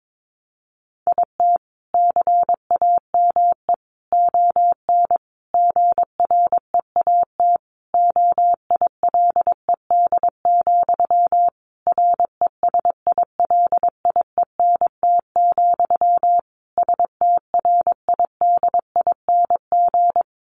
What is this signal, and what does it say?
Morse code, heard as a keyed tone, about 22 wpm, 700 Hz. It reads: IT CAME ON GREAT OILED, RESILIENT, STRIDING
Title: A Sound Of Thunder 22wpm